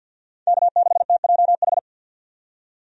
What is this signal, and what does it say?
Morse code, heard as a keyed tone, about 48 wpm, 700 Hz. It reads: K6TJH
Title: USA Call Signs - 5 Characters - 50wpm